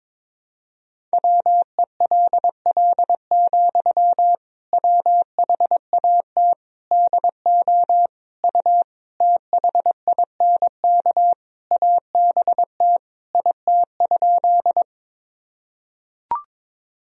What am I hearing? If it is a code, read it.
Morse code, heard as a keyed tone, about 22 wpm, 700 Hz. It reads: WELL, WHAT DO U THINK ABT IT?
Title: Sentences from Top 200 Words and Abbreviations 22wpm